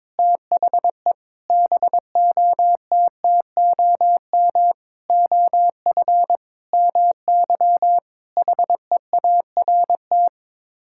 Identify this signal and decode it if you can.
Morse code, heard as a keyed tone, about 22 wpm, 700 Hz. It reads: THE BOTTOM OF MY HEART
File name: practice / 22wpm / Sets of 8 Words - 22wpm